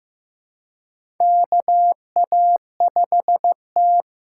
Morse code, heard as a keyed tone, about 15 wpm, 700 Hz. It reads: KA5T
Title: USA Call Signs - 4 Characters - 4x Character Spacing - 15wpm